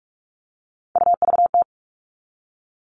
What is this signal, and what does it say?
Morse code, heard as a keyed tone, about 44 wpm, 700 Hz. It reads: UVT
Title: All Cut Numbers - 3-Digit 45wpm